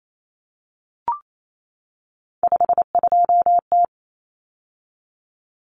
Morse code, heard as a keyed tone, about 28 wpm, 700 Hz. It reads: 52T
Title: Common Cut Numbers - 3-Digit 28wpm